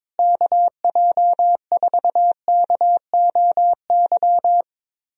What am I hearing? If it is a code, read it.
Morse code, heard as a keyed tone, about 22 wpm, 700 Hz. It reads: KJ4KOY
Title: USA Call Signs - 6 Characters - Repeated 3x - 22wpm